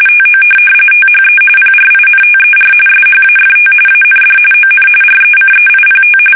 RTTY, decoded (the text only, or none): TO THE PRINTER SHAFT AND ON RECEIPT